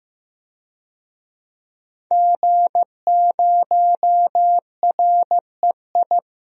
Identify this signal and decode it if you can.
Morse code, heard as a keyed tone, about 15 wpm, 700 Hz. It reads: G0REI
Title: DXCC-100 Call Signs - 5 Characters - 15wpm